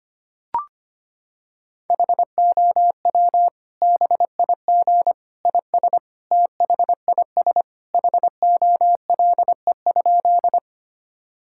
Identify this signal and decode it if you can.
Morse code, heard as a keyed tone, about 25 wpm, 700 Hz. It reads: HOW BIG IS THIS HOLE?
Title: Sentences from Top 1600 Words 25wpm